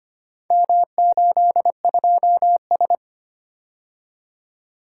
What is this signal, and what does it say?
Morse code, heard as a keyed tone, about 25 wpm, 700 Hz. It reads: M82S
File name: practice / 25wpm / Four Letter-Number Combinations - 25wpm